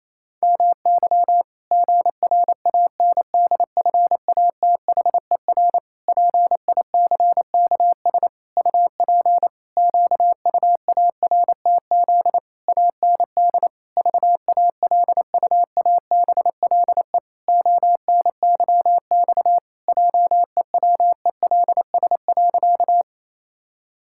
Morse code, heard as a keyed tone, about 28 wpm, 700 Hz. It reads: MY GRANDFATHER PICKS UP QUARTZ AND VALUABLE ONYX JEWELS.
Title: Pangram Challenge v2 - 28wpm